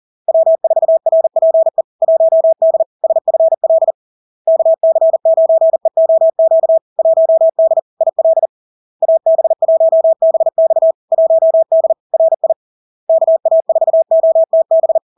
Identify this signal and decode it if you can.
Morse code, heard as a keyed tone, about 40 wpm, 650 Hz. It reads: W4RPE 1D SFL KC9EOQ 1D IL AB1BX 1D RI KA4OTB
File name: LARK-FD-2022-Log-in-CW-40wpm-650-Hz